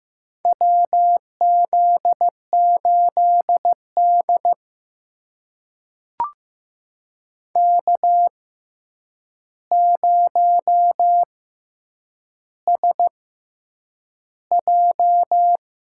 Morse code, heard as a keyed tone, about 15 wpm, spaced as for 8 wpm, 700 Hz. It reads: WZ8D K0SJ
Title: USA Call Signs - 4 Characters - 6x Character Spacing - 15wpm